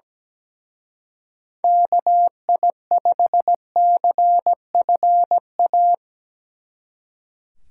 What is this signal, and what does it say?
Morse code, heard as a keyed tone, about 18 wpm, 700 Hz. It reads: KI5CFA
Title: Speed Racing - USA 2x3 Call Signs 17wpm